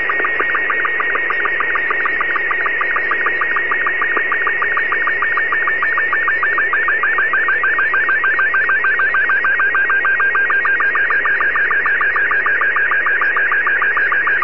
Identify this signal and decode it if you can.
SSTV (slow-scan television), Robot 36.
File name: seeds 03may08 1015utc pa3guo fm 437.485